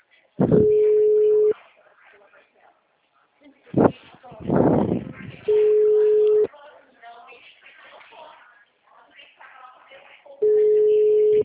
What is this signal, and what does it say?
Ringback tone (Europe, 425 Hz), with speech.
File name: out- -2005- - - .46280